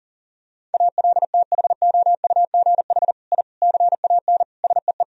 Morse code, heard as a keyed tone, about 40 wpm, 700 Hz. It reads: ALTHOUGH I CAN SEE
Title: Sentences from Top 1000 Words - Review Part 3 - 40wpm